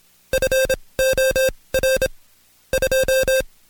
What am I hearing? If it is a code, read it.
Morse code, heard as a keyed tone, about 26 wpm, spaced as for 20 wpm, 530 Hz. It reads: FOR 2
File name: Code Quick / AudioDownld2 / Seg136-Stepper9 19WPM